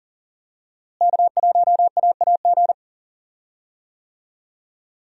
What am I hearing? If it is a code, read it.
Morse code, heard as a keyed tone, about 40 wpm, 700 Hz. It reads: K1AAG
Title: USA Call Signs - 5 Characters - Repeated 3x - 40wpm